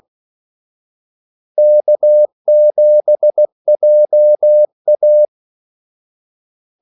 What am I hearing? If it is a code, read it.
Morse code, heard as a keyed tone, about 16 wpm, 600 Hz. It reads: K7JA
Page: Kaldesignaler 16 wpm